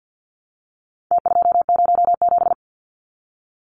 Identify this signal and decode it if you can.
Morse code, heard as a keyed tone, about 50 wpm, 700 Hz. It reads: T207
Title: Numbers - Contest Style - 4-Digit 50wpm